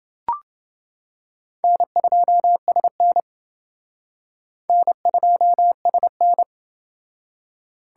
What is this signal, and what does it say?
Morse code, heard as a keyed tone, about 30 wpm, 700 Hz. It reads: N2SN N2SN